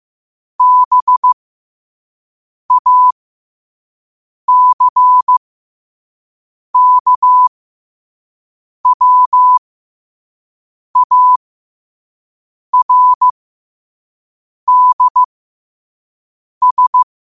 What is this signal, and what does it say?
Morse code, heard as a keyed tone, about 15 wpm, spaced as for 5 wpm, 1000 Hz. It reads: BACKWARDS
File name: qotdfiles / 5wpm